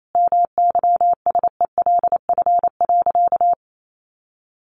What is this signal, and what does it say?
Morse code, heard as a keyed tone, about 28 wpm, 700 Hz. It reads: MYSELF.
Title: Sentences from Top 1300 Words 28wpm